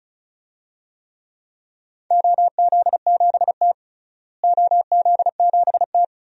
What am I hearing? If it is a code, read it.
Morse code, heard as a keyed tone, about 34 wpm, 700 Hz. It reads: OZ7T OZ7T
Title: DXCC-100 Call Signs - 4 Characters - Repeated 3x - 35wpm